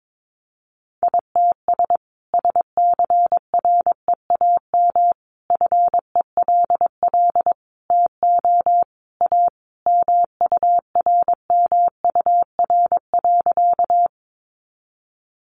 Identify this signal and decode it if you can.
Morse code, heard as a keyed tone, about 22 wpm, 700 Hz. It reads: ITS SCREAM FELL TO A MURMUR.
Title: A Sound Of Thunder 22wpm